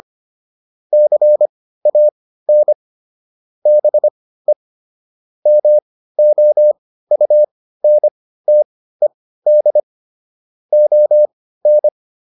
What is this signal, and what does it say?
Morse code, heard as a keyed tone, about 25 wpm, spaced as for 15 wpm, 600 Hz. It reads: CAN BE MOUNTED ON